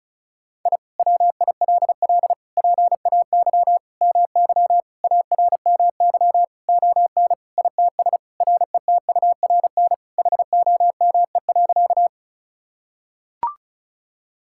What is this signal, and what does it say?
Morse code, heard as a keyed tone, about 34 wpm, 700 Hz. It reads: I WILL PAY MY ARMY ON ITS RETURN HOME.
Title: Sentences from Top 1000 Words - Review Part 5 - 35wpm